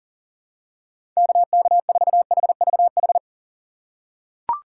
Morse code, heard as a keyed tone, about 37 wpm, 700 Hz. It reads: KK4HVH
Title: USA Call Signs - 6 Characters - 4x Character Spacing - 40wpm